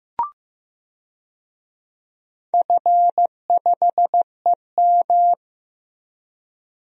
Morse code, heard as a keyed tone, about 15 wpm, 700 Hz. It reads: F5EM
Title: DXCC-100 Call Signs - 4 Characters - Repeated 3x - 15wpm